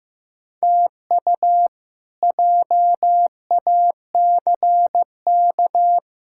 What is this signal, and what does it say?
Morse code, heard as a keyed tone, about 16 wpm, 700 Hz. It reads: TU JACK
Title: K1USN SST QSO Practice - 15wpm